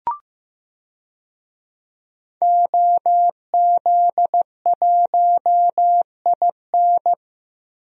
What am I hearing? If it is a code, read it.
Morse code, heard as a keyed tone, about 15 wpm, 700 Hz. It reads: OZ1IN
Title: DXCC-100 Call Signs - 5 Characters - 15wpm